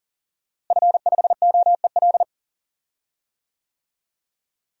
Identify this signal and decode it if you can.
Morse code, heard as a keyed tone, about 38 wpm, 700 Hz. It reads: F5OEL
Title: DXCC-100 Call Signs - 5 Characters - Repeated 2x - 40wpm